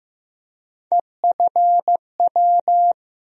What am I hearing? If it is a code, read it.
Morse code, heard as a keyed tone, about 15 wpm, 700 Hz. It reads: EFW
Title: USA Call Sign Postfixes - 3 Letters 15wpm